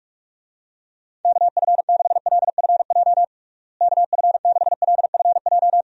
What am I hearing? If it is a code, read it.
Morse code, heard as a keyed tone, about 46 wpm, 700 Hz. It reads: KF6LFJ KF6LFJ
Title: USA Call Signs - 6 Characters - Repeated 2x - 45wpm